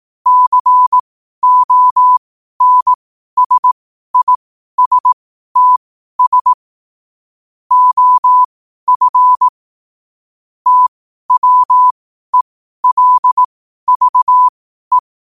Morse code, heard as a keyed tone, about 18 wpm, spaced as for 12 wpm, 1000 Hz. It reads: CONSISTS OF TWELVE
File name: qotdfiles / 12wpm